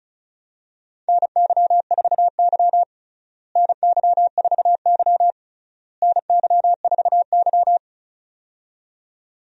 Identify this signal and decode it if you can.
Morse code, heard as a keyed tone, about 36 wpm, 700 Hz. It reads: NY4Y NY4Y NY4Y
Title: USA Call Signs - 4 Characters - Repeated 3x - 35wpm